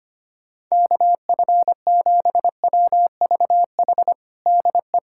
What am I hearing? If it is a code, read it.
Morse code, heard as a keyed tone, about 25 wpm, 700 Hz. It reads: KF7WVH DE
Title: USA Callsign DE Callsign - 6 Characters - 25wpm